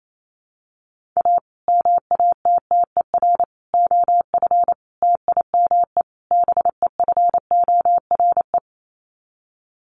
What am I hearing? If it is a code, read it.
Morse code, heard as a keyed tone, about 28 wpm, 700 Hz. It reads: A MATTER OF TIME BEFORE
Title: Sets of 5 Words Encore 28wpm